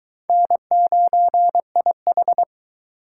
Morse code, heard as a keyed tone, about 22 wpm, 700 Hz. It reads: N9IH